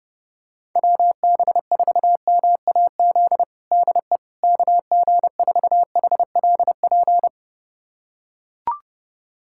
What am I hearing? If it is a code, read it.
Morse code, heard as a keyed tone, about 30 wpm, 700 Hz. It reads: WB4MAZ DE KG4HLP
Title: USA Callsign DE Callsign - 6 Characters - 30wpm